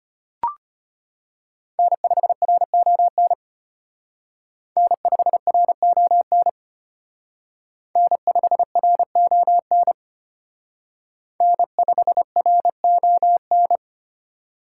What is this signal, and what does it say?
Morse code, heard as a keyed tone, about 32 wpm, 700 Hz. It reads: N5RON N5RON N5RON N5RON